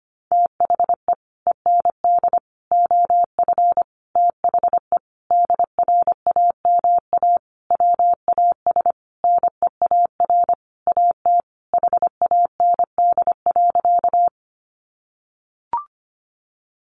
Morse code, heard as a keyed tone, about 25 wpm, 700 Hz. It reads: THE END OF THE DRAMA WAS NEAR AT HAND.
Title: Sentences from Top 1500 Words 25wpm